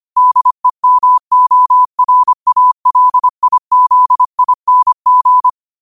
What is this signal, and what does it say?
Morse code, heard as a keyed tone, about 25 wpm, 1000 Hz. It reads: DEMORALIZING
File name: qotdfiles / 25wpm